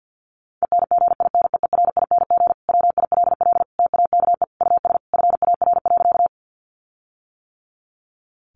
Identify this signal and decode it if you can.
Morse code, heard as a keyed tone, about 50 wpm, 700 Hz. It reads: ENGINEERING WILL TAKE US FAR.
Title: Sentences from Top 1500 Words 50wpm